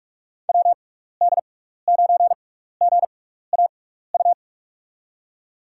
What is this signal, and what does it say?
Morse code, heard as a keyed tone, about 46 wpm, spaced as for 15 wpm, 700 Hz. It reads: WD9GAU
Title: USA Call Signs - 6x Character Spacing - 45wpm